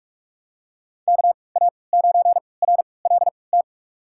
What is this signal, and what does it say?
Morse code, heard as a keyed tone, about 46 wpm, spaced as for 25 wpm, 700 Hz. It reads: KA9RLT
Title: USA Call Signs - 3x Character Spacing - 45wpm